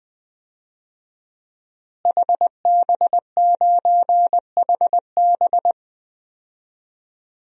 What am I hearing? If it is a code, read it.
Morse code, heard as a keyed tone, about 20 wpm, 700 Hz. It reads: HB9HB
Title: DXCC-100 Call Signs - 5 Characters - Repeated 2x - 20wpm